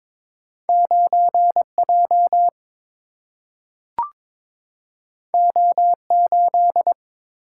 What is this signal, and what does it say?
Morse code, heard as a keyed tone, about 22 wpm, 700 Hz. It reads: 9J O8
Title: Single-Letter Single-Number 22wpm